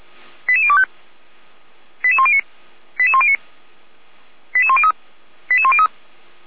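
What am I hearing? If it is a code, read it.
Five-tone selective call (ZVEI), heard as quick runs of five tones, tones of about 70 ms. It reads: 88025 88109 88109 88103 88103
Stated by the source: ZVEI five-tone call 88102 on channel 455, received by SDR at Feuerwehr Orlamünde fire service.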